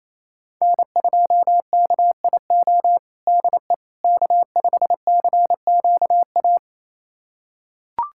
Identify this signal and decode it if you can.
Morse code, heard as a keyed tone, about 28 wpm, 700 Hz. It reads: N2KIO DE K5CQA
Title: USA Callsign DE Callsign - 5 Characters - 28wpm